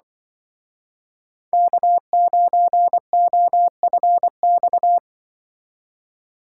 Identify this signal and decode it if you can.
Morse code, heard as a keyed tone, about 23 wpm, 700 Hz. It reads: K9OFX
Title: Speed Racing - USA 1x3 Call Signs 20wpm